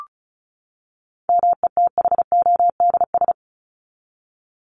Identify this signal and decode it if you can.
Morse code, heard as a keyed tone, about 34 wpm, 700 Hz. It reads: METHODS
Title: ES:S Suffix - Top 100 Words with Base Words 35wpm